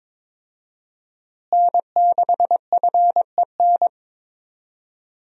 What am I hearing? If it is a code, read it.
Morse code, heard as a keyed tone, about 22 wpm, 700 Hz. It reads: N6FEN
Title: USA Call Signs - 5 Characters - Repeated 3x - 22wpm